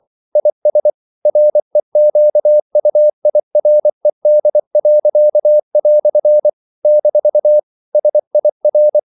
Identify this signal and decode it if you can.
Morse code, heard as a keyed tone, about 24 wpm, 600 Hz. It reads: IS REQUIRED." - SIR
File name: CW / Practice / EnglishQuotations0424